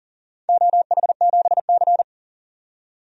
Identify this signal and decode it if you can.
Morse code, heard as a keyed tone, about 38 wpm, 700 Hz. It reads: OH7C
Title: DXCC-100 Call Signs - 4 Characters - 40wpm